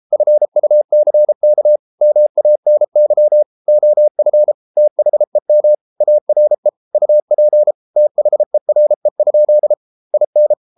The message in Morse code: FUCK MANY OF THEM ARE UP THERE? IN

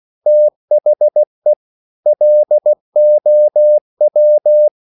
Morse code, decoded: THE LOW

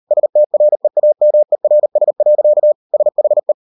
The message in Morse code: STREAMERS. SHE

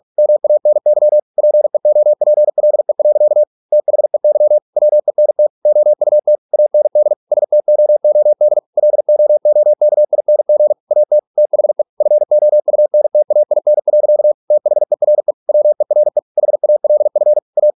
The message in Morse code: MANY PEOPLE. THEY WENT OUT AND STOOD LOOKING AT THE FOUNTAIN. THERE WERE HALF A

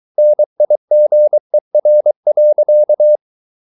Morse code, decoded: NIGER.

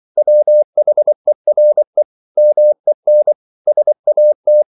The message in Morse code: WHERE MEN SAT

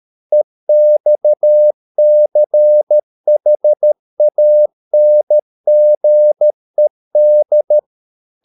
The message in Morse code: EXCHANGED

